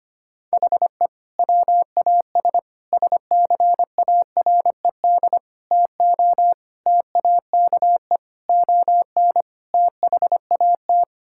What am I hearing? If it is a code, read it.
Morse code, heard as a keyed tone, about 25 wpm, 700 Hz. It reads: HE WAS SCARED TO TAKE ON THAT
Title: Sentences from Top 1000 Words - Review Part 6 - 25wpm